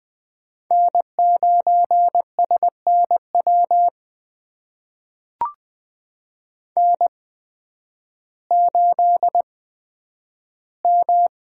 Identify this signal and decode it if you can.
Morse code, heard as a keyed tone, about 20 wpm, spaced as for 11 wpm, 700 Hz. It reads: N9SNW N8M
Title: USA Call Signs - 5 Characters - 8x Character Spacing - 20wpm